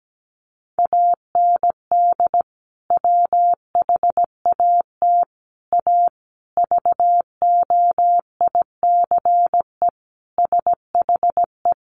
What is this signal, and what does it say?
Morse code, heard as a keyed tone, about 17 wpm, 700 Hz. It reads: AND WHAT A VOICE SHE
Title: Sentences from Top 500 Words 17wpm